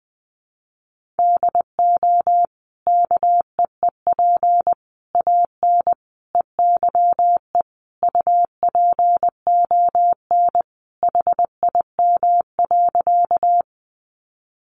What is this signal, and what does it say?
Morse code, heard as a keyed tone, about 20 wpm, 700 Hz. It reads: DO KEEP AN EYE UPON HIM.
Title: Sentences from Top 1100 Words 20wpm